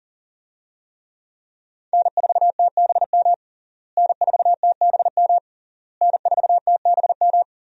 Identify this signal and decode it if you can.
Morse code, heard as a keyed tone, about 40 wpm, 700 Hz. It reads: N4TBM N4TBM N4TBM
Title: USA Call Signs - 5 Characters - Repeated 3x - 40wpm